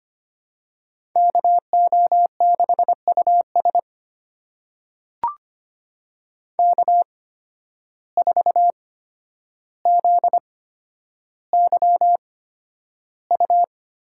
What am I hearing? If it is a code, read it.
Morse code, heard as a keyed tone, about 25 wpm, spaced as for 11 wpm, 700 Hz. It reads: KO6US K4ZYU
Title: USA Call Signs - 5 Characters - 8x Character Spacing - 25wpm